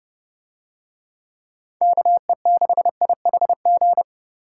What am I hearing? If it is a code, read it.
Morse code, heard as a keyed tone, about 29 wpm, 700 Hz. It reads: KE6IHG